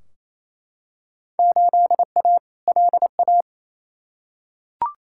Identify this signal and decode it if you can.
Morse code, heard as a keyed tone, about 28 wpm, 700 Hz. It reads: 8A LA
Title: Field Day Exchanges Encore - 28wpm